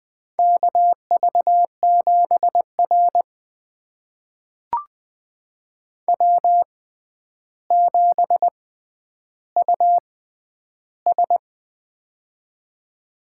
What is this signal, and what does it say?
Morse code, heard as a keyed tone, about 20 wpm, spaced as for 10 wpm, 700 Hz. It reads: KV7R W7US
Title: USA Call Signs - 4 Characters - 6x Character Spacing - 20wpm